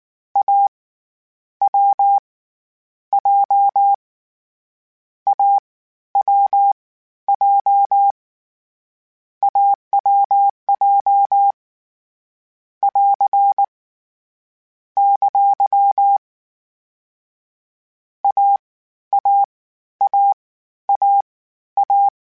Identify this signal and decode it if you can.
Morse code, heard as a keyed tone, about 19 wpm, spaced as for 12 wpm, 800 Hz. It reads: A W J AWJ AWJ <AR> ! AAAAA
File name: Difficult Combos / Difficult Combos-mp3s / AWJ1 @18